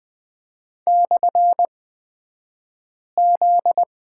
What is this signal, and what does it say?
Morse code, heard as a keyed tone, about 20 wpm, spaced as for 5 wpm, 700 Hz. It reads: /Z
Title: 249 Lesson 40d - BK - Two Characters 20wpm